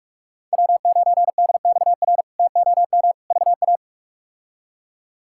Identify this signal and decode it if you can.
Morse code, heard as a keyed tone, about 44 wpm, 700 Hz. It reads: W9DXR TOM VA
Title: K1USN SST QSO Practice - 45wpm